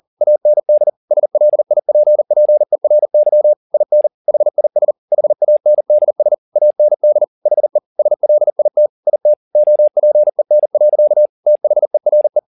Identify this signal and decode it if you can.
Morse code, heard as a keyed tone, about 40 wpm, 600 Hz. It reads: AND SLIPPERY IN HIS HANDS AND HE SLIT IT OPEN. THERE
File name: the old man and the sea 2 40wpm